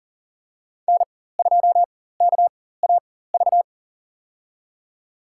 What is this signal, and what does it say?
Morse code, heard as a keyed tone, about 39 wpm, spaced as for 18 wpm, 700 Hz. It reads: N2KAV
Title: USA Call Signs - 5 Characters - 4x Character Spacing - 40wpm